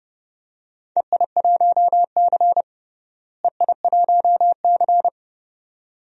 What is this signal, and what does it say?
Morse code, heard as a keyed tone, about 29 wpm, 700 Hz. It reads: EI1C EI1C
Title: DXCC-100 Call Signs - 4 Characters - Repeated 2x - 30wpm